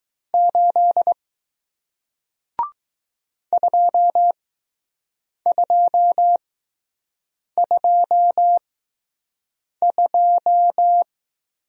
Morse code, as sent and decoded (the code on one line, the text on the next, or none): ---.. / ..--- ..--- ..--- ..---
8 2222